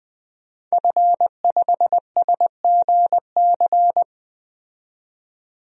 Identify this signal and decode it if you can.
Morse code, heard as a keyed tone, about 21 wpm, 700 Hz. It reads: F5SGC
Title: DXCC-100 Call Signs - 5 Characters - 20wpm